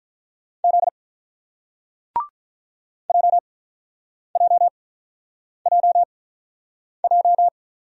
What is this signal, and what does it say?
Morse code, heard as a keyed tone, about 48 wpm, spaced as for 9 wpm, 690 Hz. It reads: Z JJJJ